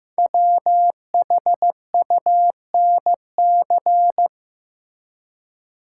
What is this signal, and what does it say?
Morse code, heard as a keyed tone, about 15 wpm, 700 Hz. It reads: WHUNC
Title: Code Groups - 5 Letters - 15wpm